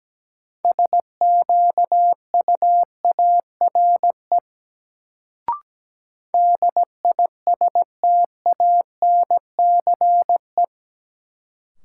Morse code, heard as a keyed tone, about 17 wpm, 700 Hz. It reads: SQUARE DISTANCE
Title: Sentences from Top 1200 Words - New Word Drill Alt - 17wpm